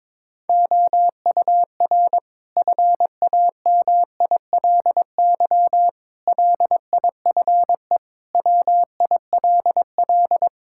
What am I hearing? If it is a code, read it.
Morse code, heard as a keyed tone, about 22 wpm, 700 Hz. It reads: OUR FAMILY LIFE WILL
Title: Sentences from Top 1000 Words - Review Part 5 - 22wpm